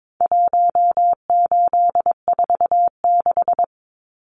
Morse code, heard as a keyed tone, about 22 wpm, 700 Hz. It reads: 1846
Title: Numbers - Contest Style - 4-Digit 22wpm